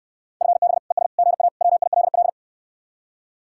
Morse code, heard as a keyed tone, about 69 wpm, 700 Hz. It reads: FB ES GA WILL